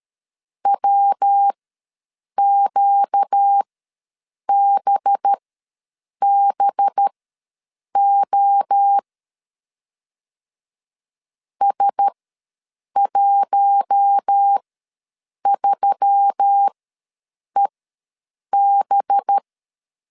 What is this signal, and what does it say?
Morse code, heard as a keyed tone, about 13 wpm, spaced as for 7 wpm, 800 Hz. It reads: WQBBO S13EB
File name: 5 LB ws 12wpm